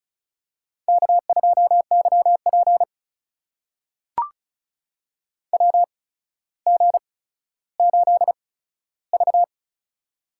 Morse code, heard as a keyed tone, about 36 wpm, spaced as for 15 wpm, 700 Hz. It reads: K2YP WG8V
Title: USA Call Signs - 4 Characters - 8x Character Spacing - 35wpm